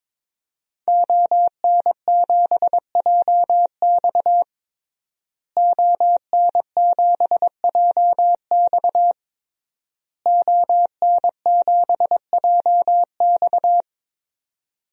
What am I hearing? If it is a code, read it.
Morse code, heard as a keyed tone, about 22 wpm, 700 Hz. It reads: ON7JX ON7JX ON7JX
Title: DXCC-100 Call Signs - 5 Characters - Repeated 3x - 22wpm